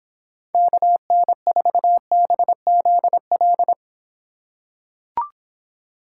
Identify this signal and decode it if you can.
Morse code, heard as a keyed tone, about 25 wpm, 700 Hz. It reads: KN4BZL